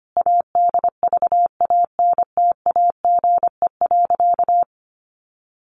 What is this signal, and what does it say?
Morse code, heard as a keyed tone, about 25 wpm, 700 Hz. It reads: ADVANTAGE.
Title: Sentences from Top 900 Words 25wpm